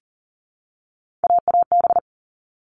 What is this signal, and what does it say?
Morse code, heard as a keyed tone, about 40 wpm, 700 Hz. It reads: AAB
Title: All Cut Numbers - 3-Digit 40wpm